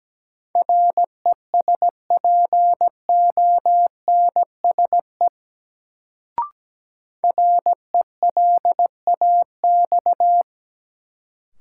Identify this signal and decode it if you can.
Morse code, heard as a keyed tone, about 17 wpm, 700 Hz. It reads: RESPONSE RELAX
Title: Sentences from Top 1400 Words - New Word Drill - 17wpm